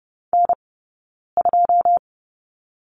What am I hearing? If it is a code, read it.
Morse code, heard as a keyed tone, about 30 wpm, 700 Hz. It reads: N 2
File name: Lesson 18d - Letter m - Two Characters 30wpm